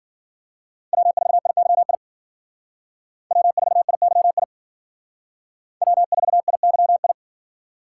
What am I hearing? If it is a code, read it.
Morse code, heard as a keyed tone, about 51 wpm, 700 Hz. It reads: W4IYI W4IYI W4IYI